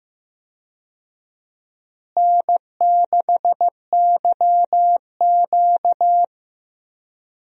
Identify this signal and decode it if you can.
Morse code, heard as a keyed tone, about 15 wpm, 700 Hz. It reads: N6YQ
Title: USA Call Signs - 4 Characters - Repeated 3x - 15wpm